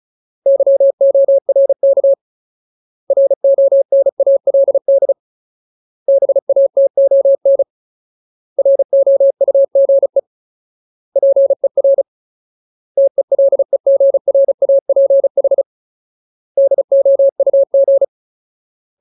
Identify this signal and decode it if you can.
Morse code, heard as a keyed tone, about 34 wpm, 550 Hz. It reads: YORK RONALD BATON ROUGE PER TELEGRAPH DOUG
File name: Ultimate Word List / 35WPM Ultimate